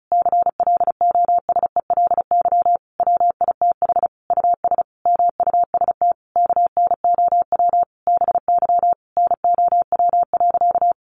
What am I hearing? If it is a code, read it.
Morse code, heard as a keyed tone, about 34 wpm, 700 Hz. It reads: CLOSELY WITH US MUST KNOW BY NOW.
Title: Sentences from Top 1400 Words 35wpm